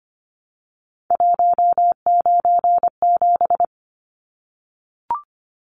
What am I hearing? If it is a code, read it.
Morse code, heard as a keyed tone, about 25 wpm, 700 Hz. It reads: 197
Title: Numbers - 3-Digit 25wpm